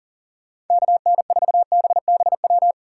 Morse code, heard as a keyed tone, about 50 wpm, 700 Hz. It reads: KN4BBW
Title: Speed Racing - USA 2x3 Call Signs 40wpm